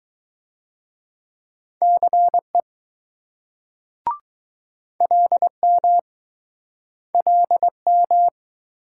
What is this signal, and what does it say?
Morse code, heard as a keyed tone, about 21 wpm, 700 Hz. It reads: CE LM LM